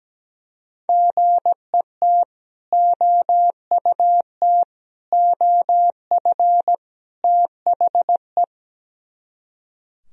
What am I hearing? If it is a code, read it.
Morse code, heard as a keyed tone, about 19 wpm, 700 Hz. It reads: GET OUT OF THE
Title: Speed-Racing - Sets of 4 Words 17wpm